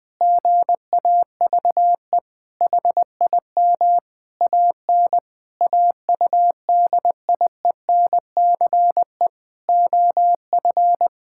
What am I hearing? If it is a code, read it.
Morse code, heard as a keyed tone, about 20 wpm, 700 Hz. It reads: GAVE HIM AN AUDIENCE OF
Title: Sentences from Top 1000 Words - Review Part 5 - 20wpm